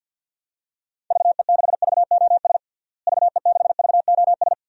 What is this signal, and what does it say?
Morse code, heard as a keyed tone, about 51 wpm, 700 Hz. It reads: VE6VOS VE6VOS
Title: DXCC-100 Call Signs - 6 Characters - Repeated 2x - 50wpm